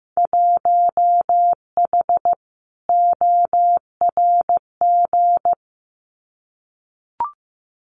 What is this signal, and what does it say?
Morse code, heard as a keyed tone, about 15 wpm, 700 Hz. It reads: 1H ORG
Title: Winter Field Day Exchanges 15wpm